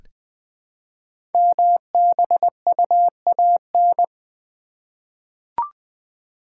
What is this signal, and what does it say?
Morse code, heard as a keyed tone, about 20 wpm, 700 Hz. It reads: MBUAN
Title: Code Groups - 5 Letters - 20wpm